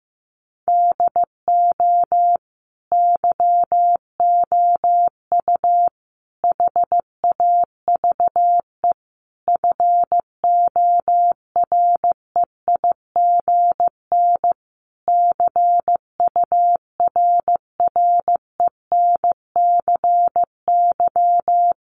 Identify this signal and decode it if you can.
Morse code, heard as a keyed tone, about 15 wpm, 700 Hz. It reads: DO YOU HAVE FOREIGN CURRENCY
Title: Sentences from Top 1500 Words 15wpm